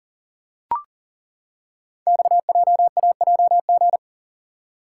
Morse code, heard as a keyed tone, about 42 wpm, 700 Hz. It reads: XJAJG
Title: Code Groups - 5 Letters - 40wpm